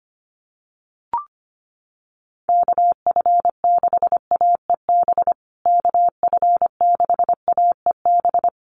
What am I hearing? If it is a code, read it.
Morse code, heard as a keyed tone, about 25 wpm, 700 Hz. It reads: KF6AEB KF6AEB
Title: USA Call Signs Repeated 3x 25wpm